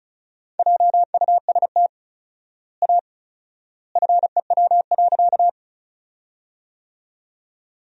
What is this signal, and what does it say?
Morse code, heard as a keyed tone, about 36 wpm, 700 Hz. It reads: JUST A FEW.
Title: 4x Word-Spacing - Sentences from Top 300 Words - 35wpm